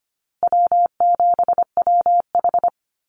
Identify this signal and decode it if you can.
Morse code, heard as a keyed tone, about 25 wpm, 700 Hz. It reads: W7WH
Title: USA Call Signs Repeated 2x 25wpm